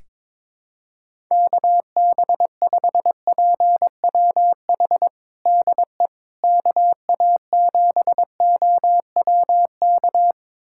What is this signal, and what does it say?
Morse code, heard as a keyed tone, about 22 wpm, 700 Hz. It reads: KB5PWH DE KA7OWK
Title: USA Callsign DE Callsign 22wpm